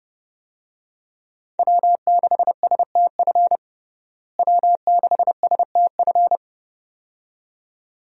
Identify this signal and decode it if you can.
Morse code, heard as a keyed tone, about 30 wpm, 700 Hz. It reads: W6STF W6STF
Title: USA Call Signs - 5 Characters - Repeated 2x - 30wpm